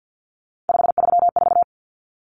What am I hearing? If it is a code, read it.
Morse code, heard as a keyed tone, about 50 wpm, 700 Hz. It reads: 534
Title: Numbers - 3-Digit 50wpm